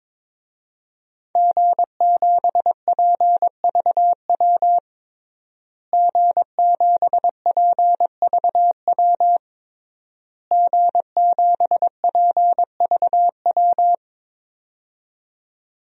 Morse code, heard as a keyed tone, about 22 wpm, 700 Hz. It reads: G7PVW G7PVW G7PVW
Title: DXCC-100 Call Signs - 5 Characters - Repeated 3x - 22wpm